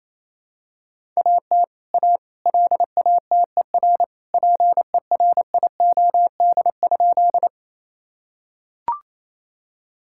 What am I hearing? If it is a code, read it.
Morse code, heard as a keyed tone, about 28 wpm, 700 Hz. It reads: AT A LATER PERIOD?
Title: Sentences from Top 1000 Words - Review Part 5 - 28wpm